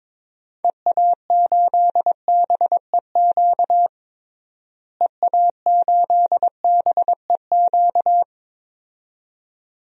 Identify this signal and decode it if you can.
Morse code, heard as a keyed tone, about 21 wpm, 700 Hz. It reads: EA8BEQ EA8BEQ
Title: DXCC-100 Call Signs - 6 Characters - Repeated 2x - 22wpm